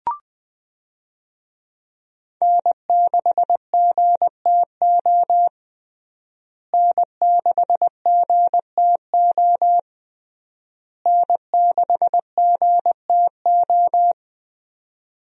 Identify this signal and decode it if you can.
Morse code, heard as a keyed tone, about 20 wpm, 700 Hz. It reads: N6GTO N6GTO N6GTO
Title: USA Call Signs - 5 Characters - Repeated 3x - 20wpm